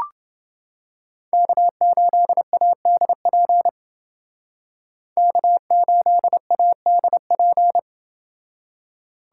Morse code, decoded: K8ADP K8ADP

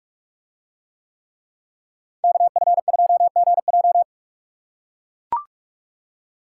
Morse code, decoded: KF2GJ

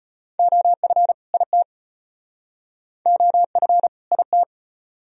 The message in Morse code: OF IT OF IT

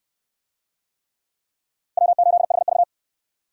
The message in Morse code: W8HV

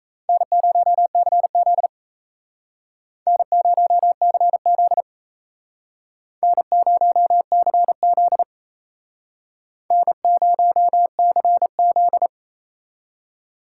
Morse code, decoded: N0CZ N0CZ N0CZ N0CZ

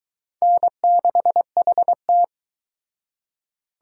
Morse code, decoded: N6HT

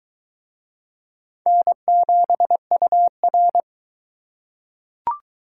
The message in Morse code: N7UR

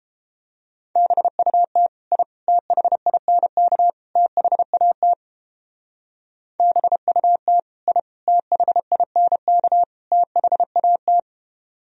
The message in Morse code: BUT I THINK THAT BUT I THINK THAT

